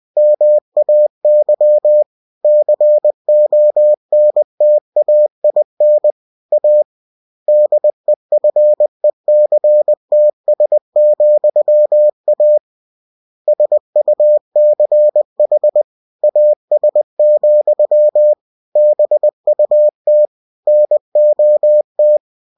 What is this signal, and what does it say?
Morse code, heard as a keyed tone, about 20 wpm, 600 Hz. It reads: MAY CONTAIN A DEFECTS,A SUCH AS, BUT NOT